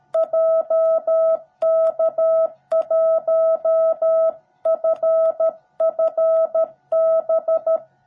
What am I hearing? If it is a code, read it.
Morse code, heard as a keyed tone, about 13 wpm, 660 Hz. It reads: JK1FFB
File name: jh8cbh / 2021cw / cw17b